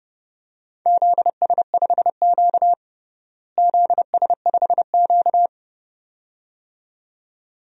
Morse code, heard as a keyed tone, about 30 wpm, 700 Hz. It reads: ZS5Q ZS5Q
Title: DXCC-100 Call Signs - 4 Characters - Repeated 2x - 30wpm